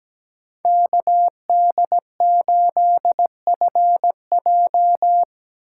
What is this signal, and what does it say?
Morse code, heard as a keyed tone, about 17 wpm, 700 Hz. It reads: KD8FJ
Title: USA Call Signs - 5 Characters - 2x Character Spacing - 17wpm